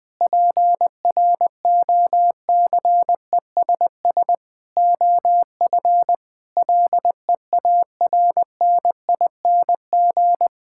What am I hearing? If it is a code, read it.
Morse code, heard as a keyed tone, about 20 wpm, 700 Hz. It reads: PROCESS OF LEARNING
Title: Sentences with 3-Word Phrases - 20wpm